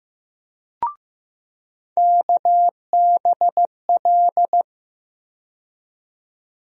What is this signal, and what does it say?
Morse code, heard as a keyed tone, about 15 wpm, 700 Hz. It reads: KBL
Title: USA Call Sign Postfixes - 3 Letters 15wpm